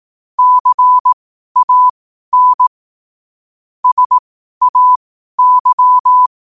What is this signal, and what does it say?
Morse code, heard as a keyed tone, about 18 wpm, spaced as for 12 wpm, 1000 Hz. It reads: CAN SAY
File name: qotdfiles / 12wpm